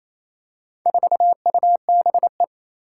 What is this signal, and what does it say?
Morse code, heard as a keyed tone, about 27 wpm, 700 Hz. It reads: 4UBE
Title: Four Challenging Letter-Number Combinations 28wpm